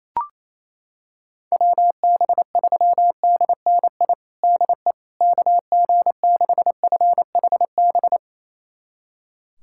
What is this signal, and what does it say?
Morse code, heard as a keyed tone, about 28 wpm, 700 Hz. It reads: WB3DNI DE KG6FHB
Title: USA Callsign DE Callsign - 6 Characters - 28wpm